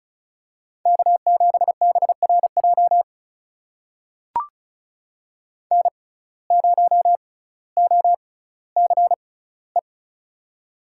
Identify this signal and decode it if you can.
Morse code, heard as a keyed tone, about 34 wpm, spaced as for 18 wpm, 700 Hz. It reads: K7BRJ N0OCE
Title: USA Call Signs - 5 Characters - 6x Character Spacing - 35wpm